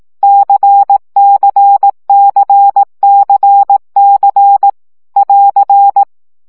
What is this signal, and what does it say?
Morse code, heard as a keyed tone, about 18 wpm, 800 Hz. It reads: CCCCC <AR>
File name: CW gyakorlas c 15 baud